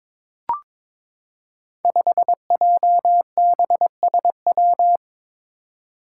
Morse code, heard as a keyed tone, about 21 wpm, 700 Hz. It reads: 5JBSW
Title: Five Challenging Letter-Number Combinations 22wpm